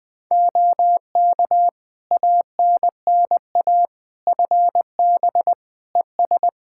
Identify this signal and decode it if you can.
Morse code, heard as a keyed tone, about 20 wpm, 700 Hz. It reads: OK ANNA FB ES
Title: Practice QSO - Exchange 2 - Info - 20wpm